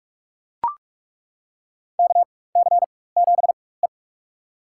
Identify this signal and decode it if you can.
Morse code, heard as a keyed tone, about 43 wpm, spaced as for 21 wpm, 690 Hz. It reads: KC7E
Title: USA Call Signs - 4 Characters - 4x Character Spacing - 45wpm